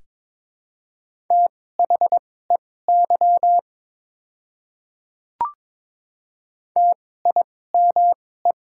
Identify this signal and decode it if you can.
Morse code, heard as a keyed tone, about 23 wpm, spaced as for 16 wpm, 700 Hz. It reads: THEY TIME
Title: 2x Character Spacing - Top 100 Words - 22wpm